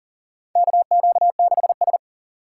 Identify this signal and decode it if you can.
Morse code, heard as a keyed tone, about 40 wpm, 700 Hz. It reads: KQ6S
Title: USA Call Signs - 6x Character Spacing - 40wpm